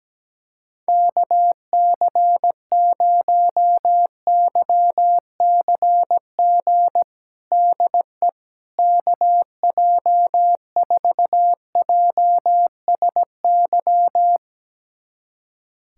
Morse code, heard as a keyed tone, about 17 wpm, 700 Hz. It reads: KC0YCG DE KJ4JSY
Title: USA Callsign DE Callsign - 6 Characters - 17wpm